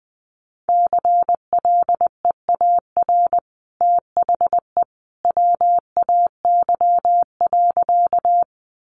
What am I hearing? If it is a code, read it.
Morse code, heard as a keyed tone, about 19 wpm, 700 Hz. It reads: CLEAR THE WAY.
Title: Sentences from Top 500 Words 20wpm